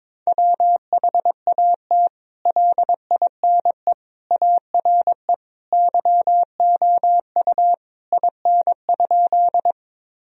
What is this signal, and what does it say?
Morse code, heard as a keyed tone, about 22 wpm, 700 Hz. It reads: WHAT LINE ARE YOU IN?
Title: Sentences from Top 1000 Words - Review Part 1 - 22wpm